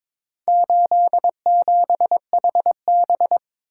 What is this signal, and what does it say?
Morse code, heard as a keyed tone, about 22 wpm, 700 Hz. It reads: 87HB
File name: practice / 22wpm / Four Letter-Number Combinations - 22wpm